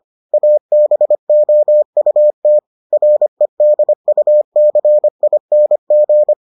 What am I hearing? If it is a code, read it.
Morse code, heard as a keyed tone, about 25 wpm, 600 Hz. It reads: ABOUT REDUCING